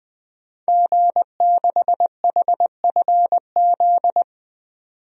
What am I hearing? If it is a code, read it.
Morse code, heard as a keyed tone, about 20 wpm, 700 Hz. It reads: G6HFZ
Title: DXCC-100 Call Signs - 5 Characters - 20wpm